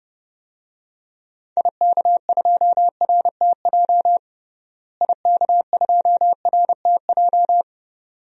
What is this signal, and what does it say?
Morse code, heard as a keyed tone, about 30 wpm, 700 Hz. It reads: IK2RTJ IK2RTJ
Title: DXCC-100 Call Signs - 6 Characters - Repeated 2x - 30wpm